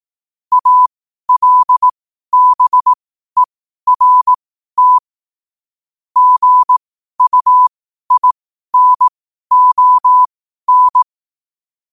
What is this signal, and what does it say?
Morse code, heard as a keyed tone, about 18 wpm, spaced as for 12 wpm, 1000 Hz. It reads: ALBERT GUINON
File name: qotdfiles / 12wpm